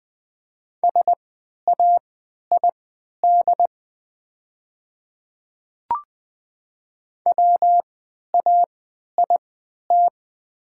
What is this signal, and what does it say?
Morse code, heard as a keyed tone, about 20 wpm, spaced as for 11 wpm, 700 Hz. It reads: SAID WAIT
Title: 3x Character Spacing - Top 500 Words - 20wpm